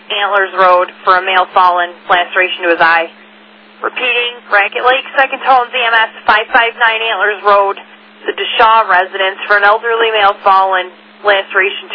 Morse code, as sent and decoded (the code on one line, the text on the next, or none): none